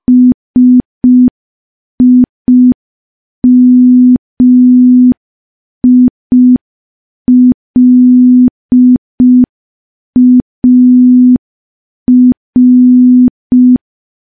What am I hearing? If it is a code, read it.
Morse code, heard as a keyed tone, about 5 wpm, 250 Hz. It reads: SIMILAR
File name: morse / morse-training-05WPM